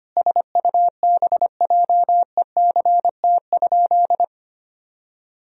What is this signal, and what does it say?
Morse code, heard as a keyed tone, about 25 wpm, 700 Hz. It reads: SUBJECT?
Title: Sentences from Top 1000 Words - Review Part 1 - 25wpm